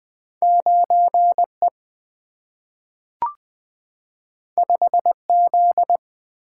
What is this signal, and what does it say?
Morse code, heard as a keyed tone, about 20 wpm, 700 Hz. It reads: 9 E 5 Z
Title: Two Letter-Number Combinations - 20wpm